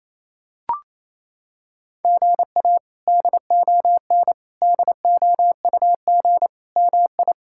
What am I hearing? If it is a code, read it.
Morse code, heard as a keyed tone, about 27 wpm, 700 Hz. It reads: GA DON DOUG MI
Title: K1USN SST QSO Practice - 28wpm